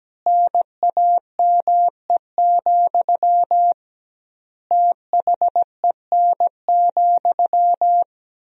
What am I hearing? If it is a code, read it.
Morse code, heard as a keyed tone, about 17 wpm, 700 Hz. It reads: NAME, THEN,
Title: 2x Word-Spacing - Sentences from Top 500 Words - 17wpm